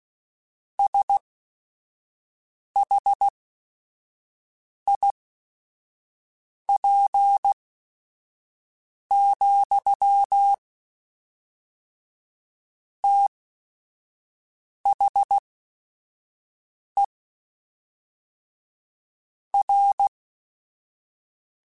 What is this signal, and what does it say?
Morse code, heard as a keyed tone, about 16 wpm, spaced as for 5 wpm, 800 Hz. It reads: SHIP, THE R
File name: Technical Subjects / Technical Subjects-mp3s / XC @5